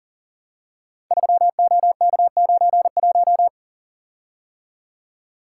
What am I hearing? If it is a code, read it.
Morse code, heard as a keyed tone, about 39 wpm, 700 Hz. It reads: 3OK91
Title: Code Groups - 5 Letter:Numbers - 40wpm